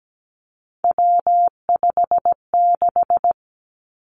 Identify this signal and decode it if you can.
Morse code, heard as a keyed tone, about 17 wpm, 700 Hz. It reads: W56
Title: Three Challenging Letter-Number Combinations 17wpm